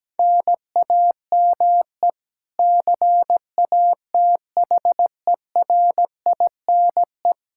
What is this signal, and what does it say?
Morse code, heard as a keyed tone, about 17 wpm, 700 Hz. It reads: NAME CATHERINE
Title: Practice QSO - Exchange 1 - Beginning - 17wpm